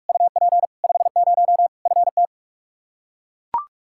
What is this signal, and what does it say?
Morse code, heard as a keyed tone, about 42 wpm, 700 Hz. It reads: UP 50 FT
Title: Practice QSO - Exchange 2 - Info - 45wpm